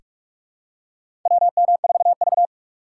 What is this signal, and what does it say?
Morse code, heard as a keyed tone, about 46 wpm, 700 Hz. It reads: WM4V
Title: USA Call Signs - 4 Characters - 3x Character Spacing - 45wpm